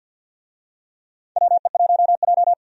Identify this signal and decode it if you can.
Morse code, heard as a keyed tone, about 53 wpm, 700 Hz. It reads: WE1J
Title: USA Call Signs - 4 Characters - 50wpm